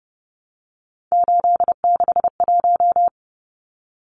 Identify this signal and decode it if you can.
Morse code, heard as a keyed tone, about 30 wpm, 700 Hz. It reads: 861
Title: Numbers - 3-Digit 30wpm